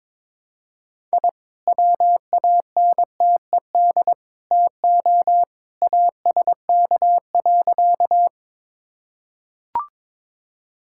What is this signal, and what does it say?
Morse code, heard as a keyed tone, about 21 wpm, 700 Hz. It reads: I WANTED TO ASK.
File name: practice / 22wpm / Sentences with 3-Word Phrases - 22wpm